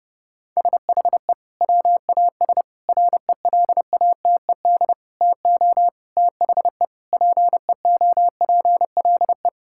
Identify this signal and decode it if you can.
Morse code, heard as a keyed tone, about 30 wpm, 700 Hz. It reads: SHE WAS RELATED TO THE PEOPLE
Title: Sentences from Top 1000 Words - Review Part 5 - 30wpm